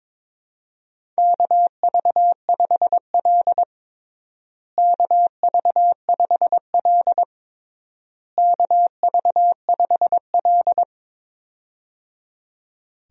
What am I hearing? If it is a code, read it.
Morse code, heard as a keyed tone, about 22 wpm, 700 Hz. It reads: KV5L KV5L KV5L
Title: USA Call Signs - 4 Characters - Repeated 3x - 22wpm